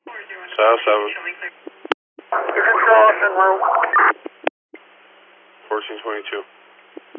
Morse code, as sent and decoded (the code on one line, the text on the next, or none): none